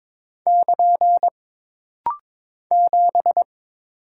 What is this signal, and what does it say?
Morse code, heard as a keyed tone, about 21 wpm, 700 Hz. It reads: <KN> 7
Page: This rapid-fire timing is geared to push your ICR (Instant Character Recognition) skills.